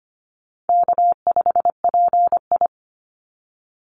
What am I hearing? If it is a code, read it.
Morse code, heard as a keyed tone, about 25 wpm, 700 Hz. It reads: K5PI
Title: Easy Call Signs Repeated 3x - 25wpm